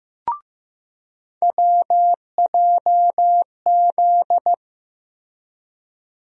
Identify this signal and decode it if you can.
Morse code, heard as a keyed tone, about 15 wpm, 700 Hz. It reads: WJZ
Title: USA Call Sign Postfixes - 3 Letters 15wpm